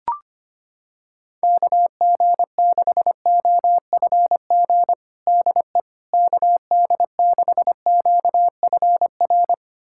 Morse code, heard as a keyed tone, about 25 wpm, 700 Hz. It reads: KG6OFG DE KD6QFR
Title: USA Callsign DE Callsign 25wpm